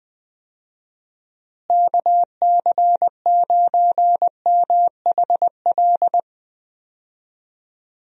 Morse code, heard as a keyed tone, about 20 wpm, 700 Hz. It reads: KC9MHL
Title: USA Call Signs - 6 Characters - Repeated 2x - 20wpm